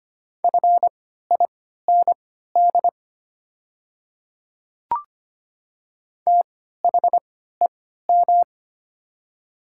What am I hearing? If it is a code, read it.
Morse code, heard as a keyed tone, about 25 wpm, spaced as for 14 wpm, 700 Hz. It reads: FIND THEM
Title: 3x Character Spacing - Top 100 Words - 25wpm